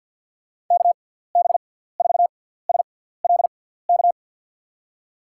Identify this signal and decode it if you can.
Morse code, heard as a keyed tone, about 53 wpm, spaced as for 17 wpm, 700 Hz. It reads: KB4SLK
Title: USA Call Signs - 6x Character Spacing - 50wpm